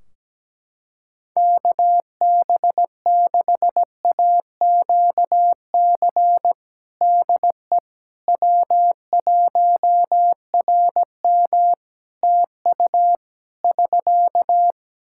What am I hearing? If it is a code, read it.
Morse code, heard as a keyed tone, about 17 wpm, 700 Hz. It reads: KB6AQC DE W1RM TU <SK>
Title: Practice QSO - Exchange 4 - Ending - 17wpm